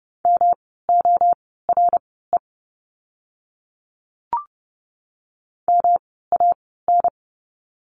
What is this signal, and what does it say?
Morse code, heard as a keyed tone, about 30 wpm, spaced as for 17 wpm, 700 Hz. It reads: MORE MAN
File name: course / 30:10wpm / 108 Lesson 18f - Letter m - Words 30wpm